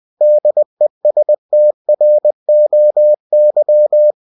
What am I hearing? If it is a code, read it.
Morse code, heard as a keyed tone, about 20 wpm, 600 Hz. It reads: DESTROY
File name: cw-books / winnie-the-pooh / winnie-20wpm-page-0117